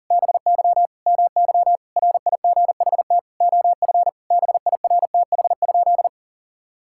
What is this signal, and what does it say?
Morse code, heard as a keyed tone, about 37 wpm, 700 Hz. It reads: BY MY RIGHT OF BIRTH?
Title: Sentences from Top 1000 Words - Review Part 6 - 40wpm